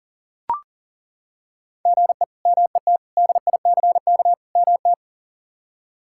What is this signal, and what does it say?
Morse code, heard as a keyed tone, about 40 wpm, 700 Hz. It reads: GE MET DICK MT
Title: K1USN SST QSO Practice - 40wpm